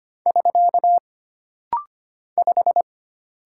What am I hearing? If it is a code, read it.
Morse code, heard as a keyed tone, about 25 wpm, 700 Hz. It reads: <SK> 5
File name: Single Character (Letter:Number:Punctuation:Prosign) - Rapid-Fire - 25wpm